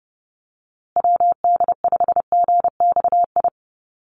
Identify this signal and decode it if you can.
Morse code, heard as a keyed tone, about 30 wpm, 700 Hz. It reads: WD5GXI
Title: USA Call Signs Repeated 2x 30wpm